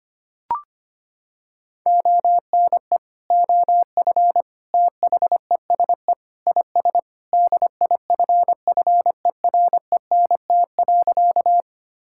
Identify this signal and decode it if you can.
Morse code, heard as a keyed tone, about 25 wpm, 700 Hz. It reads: ONE OF THESE IS DIFFERENT.
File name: practice / 25wpm / Sentences with 2-Word Phrases - 25wpm